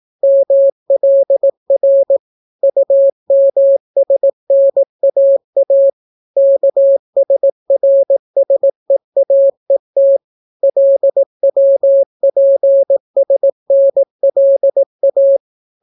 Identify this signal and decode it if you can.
Morse code, heard as a keyed tone, about 18 wpm, 550 Hz. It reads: MLR UMSNAA KSRSEAET LWPSNLA